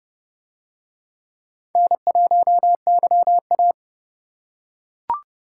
Morse code, heard as a keyed tone, about 28 wpm, 700 Hz. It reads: N1YA